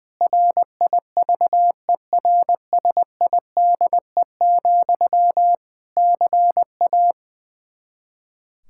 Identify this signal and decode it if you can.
Morse code, heard as a keyed tone, about 20 wpm, 700 Hz. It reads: RIVERSIDE, CA
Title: Practice QSO - Exchange 1 - Beginning - 20wpm